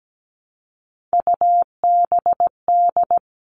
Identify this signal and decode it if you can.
Morse code, heard as a keyed tone, about 17 wpm, 700 Hz. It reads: UBD
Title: All Cut Numbers - 3-Digit 17wpm